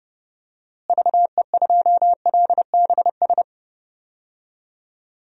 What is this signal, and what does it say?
Morse code, heard as a keyed tone, about 30 wpm, 700 Hz. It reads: VE2LBS
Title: DXCC-100 Call Signs - 6 Characters - Repeated 2x - 30wpm